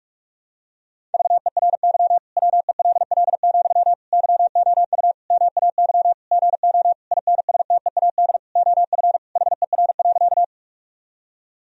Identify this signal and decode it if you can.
Morse code, heard as a keyed tone, about 46 wpm, 700 Hz. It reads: VERY WELL, YOU MAY GO INSTEAD OF HER.
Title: Sentences from Top 1000 Words - Review Part 4 - 45wpm